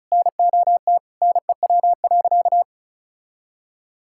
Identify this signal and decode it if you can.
Morse code, heard as a keyed tone, about 36 wpm, 700 Hz. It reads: NOT NEW.
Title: Sentences from Top 1600 Words 35wpm